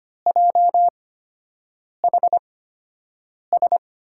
Morse code, heard as a keyed tone, about 25 wpm, spaced as for 7 wpm, 700 Hz. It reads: JHS
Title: USA Call Signs - 6 Characters - 8x Character Spacing - 25wpm